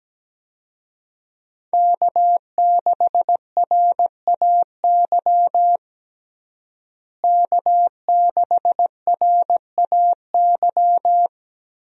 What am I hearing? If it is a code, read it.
Morse code, heard as a keyed tone, about 17 wpm, 700 Hz. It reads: K6RAY K6RAY
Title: USA Call Signs - 5 Characters - Repeated 2x - 17wpm